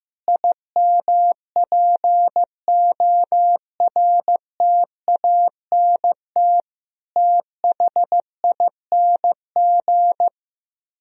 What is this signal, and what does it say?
Morse code, heard as a keyed tone, about 15 wpm, 700 Hz. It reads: IMPORTANT THING
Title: Speed-Racing - Sets of 4 Words 15wpm